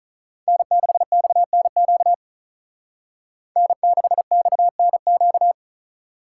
Morse code, heard as a keyed tone, about 44 wpm, 700 Hz. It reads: N6XNQ N6XNQ